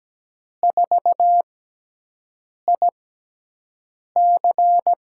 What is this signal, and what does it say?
Morse code, heard as a keyed tone, about 17 wpm, spaced as for 6 wpm, 700 Hz. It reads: 4IC
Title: USA Call Signs - 5 Characters - 6x Character Spacing - 17wpm